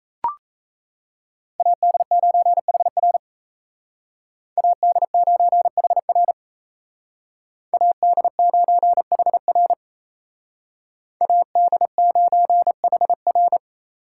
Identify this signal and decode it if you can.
Morse code, heard as a keyed tone, about 36 wpm, 700 Hz. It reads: AD9HR AD9HR AD9HR AD9HR